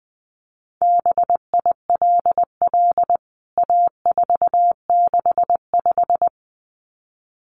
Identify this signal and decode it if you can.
Morse code, heard as a keyed tone, about 20 wpm, 700 Hz. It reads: BILL A465
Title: CWT Exchange 20wpm